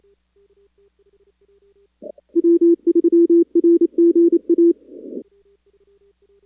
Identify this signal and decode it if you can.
Morse code, heard as a keyed tone, about 29 wpm, 340 Hz. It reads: W3RGA
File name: contests / 2019-cw-wpx-kt5j-k5pi- -2330 0-4